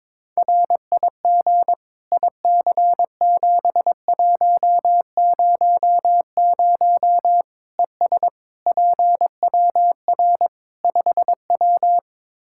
Morse code, heard as a keyed tone, about 22 wpm, 700 Hz. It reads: RIG IC7100 ES PWR 5W
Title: Practice QSO - Exchange 2 - Info - 22wpm